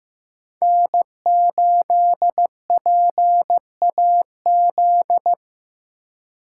Morse code, heard as a keyed tone, about 18 wpm, 700 Hz. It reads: N8PAZ
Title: Speed Racing - USA 1x3 Call Signs 15wpm